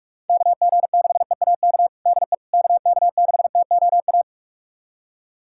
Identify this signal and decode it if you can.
Morse code, heard as a keyed tone, about 42 wpm, 700 Hz. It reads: KG6EAK DE KK6TOA
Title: USA Callsign DE Callsign - 6 Characters - 45wpm